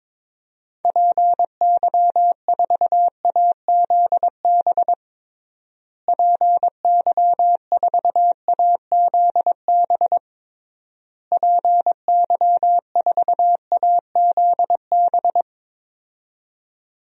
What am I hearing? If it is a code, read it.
Morse code, heard as a keyed tone, about 22 wpm, 700 Hz. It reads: PY4AZB PY4AZB PY4AZB
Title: DXCC-100 Call Signs - 6 Characters - Repeated 3x - 22wpm